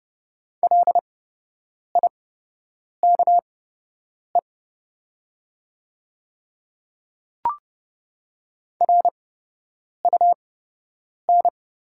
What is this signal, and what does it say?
Morse code, heard as a keyed tone, about 30 wpm, spaced as for 8 wpm, 700 Hz. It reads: LIKE RUN
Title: 8x Character Spacing - Top 300 Words - 30wpm